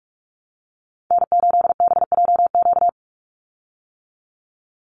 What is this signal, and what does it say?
Morse code, heard as a keyed tone, about 44 wpm, 700 Hz. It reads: N8BJQ
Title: Call Signs - 45wpm